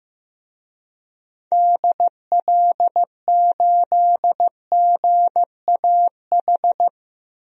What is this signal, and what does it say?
Morse code, heard as a keyed tone, about 15 wpm, 700 Hz. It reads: DL8GAH
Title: DXCC-100 Call Signs - 6 Characters - 15wpm